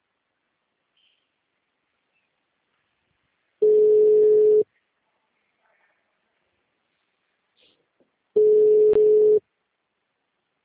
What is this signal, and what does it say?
Ringback tone (Europe, 425 Hz), with speech.